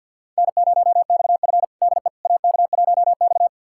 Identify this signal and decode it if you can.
Morse code, heard as a keyed tone, about 51 wpm, 700 Hz. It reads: N0XF DE AK1X
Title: USA Callsign DE Callsign - 4 Characters - 50wpm